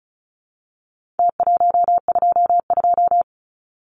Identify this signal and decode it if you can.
Morse code, heard as a keyed tone, about 34 wpm, 700 Hz. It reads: T122
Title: Numbers - Contest Style - 4-Digit 35wpm